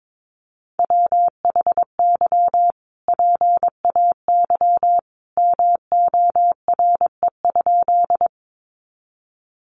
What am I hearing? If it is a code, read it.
Morse code, heard as a keyed tone, about 22 wpm, 700 Hz. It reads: WHY PAY MORE?
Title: Sentences from Top 700 Words 22wpm